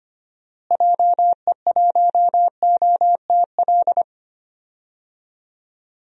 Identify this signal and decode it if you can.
Morse code, heard as a keyed tone, about 25 wpm, 700 Hz. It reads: JE1OTL
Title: DXCC-100 Call Signs - 6 Characters - Repeated 2x - 25wpm